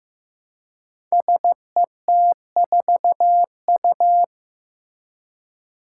Morse code, heard as a keyed tone, about 15 wpm, 700 Hz. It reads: SET4U
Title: Five Challenging Letter-Number Combinations 15wpm